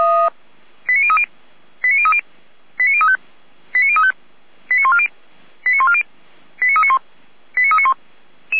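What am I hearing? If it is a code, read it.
Five-tone selective call (ZVEI), heard as quick runs of five tones, tones of about 70 ms. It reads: 89030 89030 89035 89035 89140 89140 89391 89391
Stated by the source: ZVEI five-tone call received on an SDR.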